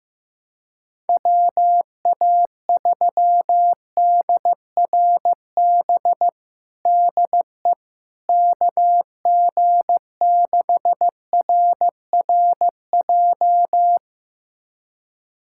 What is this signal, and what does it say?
Morse code, heard as a keyed tone, about 15 wpm, 700 Hz. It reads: WA3DRB DE KG6RRJ
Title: USA Callsign DE Callsign - 6 Characters - 15wpm